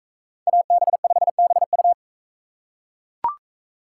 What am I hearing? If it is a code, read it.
Morse code, heard as a keyed tone, about 41 wpm, 700 Hz. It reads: AB5BU